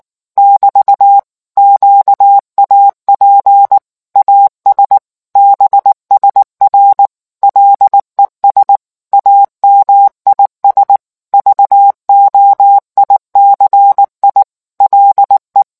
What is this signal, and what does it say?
Morse code, heard as a keyed tone, about 19 wpm, 800 Hz. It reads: <BT> QAP AS BSR LES AMIS VOICI LE